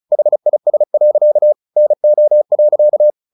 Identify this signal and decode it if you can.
Morse code, heard as a keyed tone, about 34 wpm, 600 Hz. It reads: HIS. NO.
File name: the old man and the sea 2 35wpm